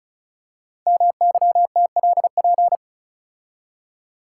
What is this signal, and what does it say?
Morse code, heard as a keyed tone, about 36 wpm, 700 Hz. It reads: MYTLP
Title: Code Groups - 5 Letters - 35wpm